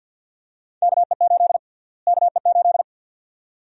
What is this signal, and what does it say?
Morse code, heard as a keyed tone, about 52 wpm, 700 Hz. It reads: KE8 KE8
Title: USA Call Sign Prefixes - Repeated 2x - 50wpm